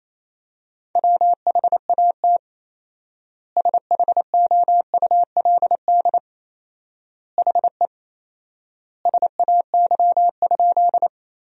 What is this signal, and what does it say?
Morse code, heard as a keyed tone, about 27 wpm, 700 Hz. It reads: WHAT SHOULD HE SAY?
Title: 4x Word-Spacing - Sentences from Top 300 Words - 28wpm